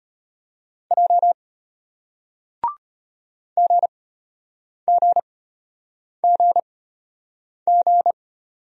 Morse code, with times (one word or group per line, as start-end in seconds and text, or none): 0.9-1.3 J
3.6-8.1 GGGG